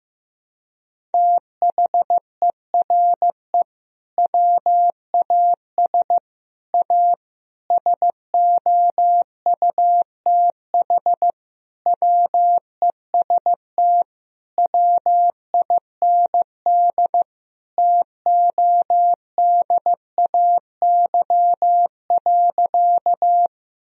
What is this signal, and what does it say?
Morse code, heard as a keyed tone, about 15 wpm, 700 Hz. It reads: THERE WAS A SOUTH WEST WIND TODAY.
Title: Sentences from Top 1000 Words - Review Part 2 - 15wpm